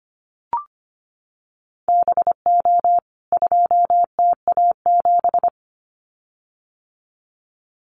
Morse code, heard as a keyed tone, about 25 wpm, 700 Hz. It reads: BO 2TA7
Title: CWT Exchange 25wpm